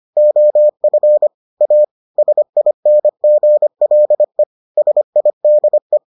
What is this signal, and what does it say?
Morse code, heard as a keyed tone, about 25 wpm, 600 Hz. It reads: OF A SINGLE SIDE